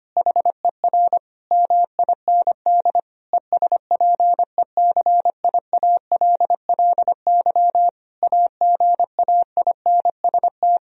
Morse code, heard as a keyed tone, about 25 wpm, 700 Hz. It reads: HER MIND ESPECIALLY AGAINST
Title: Sentences from Top 1000 Words - Review Part 4 - 25wpm